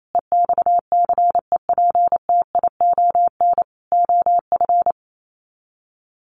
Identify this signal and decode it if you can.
Morse code, heard as a keyed tone, about 28 wpm, 700 Hz. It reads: EXCEPTION OF
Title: Sets of 4 Words 28wpm